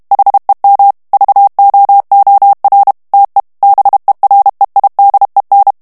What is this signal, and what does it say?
Morse code, heard as a keyed tone, about 32 wpm, 800 Hz. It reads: HEM VOOR TE BEREIDEN
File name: morse mp3 / HOLLANDAIS texte clair 2 vitesse 30 mots minute